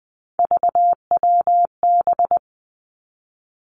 Morse code, heard as a keyed tone, about 20 wpm, 700 Hz. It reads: VWB
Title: Three Challenging Letter-Number Combinations 20wpm